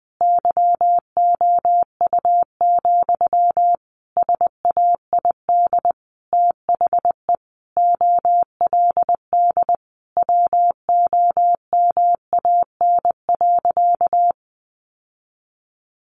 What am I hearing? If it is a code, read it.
Morse code, heard as a keyed tone, about 20 wpm, 700 Hz. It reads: YOU, SAID THE OLD WOMAN.
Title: Sentences from Top 1100 Words 20wpm